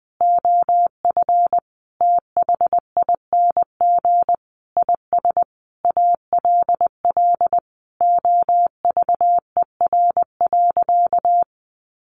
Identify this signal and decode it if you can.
Morse code, heard as a keyed tone, about 20 wpm, 700 Hz. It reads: OF THING IS ALL OVER.
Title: Sentences from Top 1300 Words 20wpm